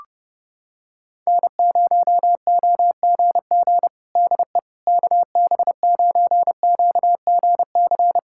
Morse code, heard as a keyed tone, about 30 wpm, 700 Hz. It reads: N0OGG DE KB9QGC
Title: USA Callsign DE Callsign 30wpm